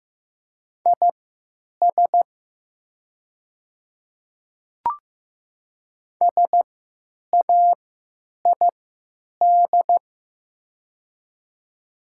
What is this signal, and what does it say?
Morse code, heard as a keyed tone, about 15 wpm, spaced as for 9 wpm, 700 Hz. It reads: IS SAID
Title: 3x Character Spacing - Top 100 Words - 15wpm